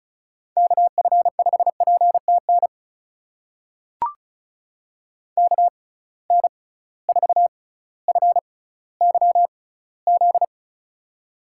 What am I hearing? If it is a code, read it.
Morse code, heard as a keyed tone, about 36 wpm, spaced as for 18 wpm, 700 Hz. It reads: KF5PTN KN4FYZ
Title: USA Call Signs - 6 Characters - 6x Character Spacing - 35wpm